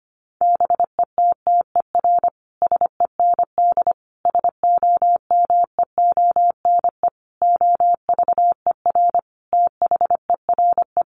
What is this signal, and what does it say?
Morse code, heard as a keyed tone, about 25 wpm, 700 Hz. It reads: BETTER SEND SOMEONE OVER THERE
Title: Sentences from Top 1500 Words 25wpm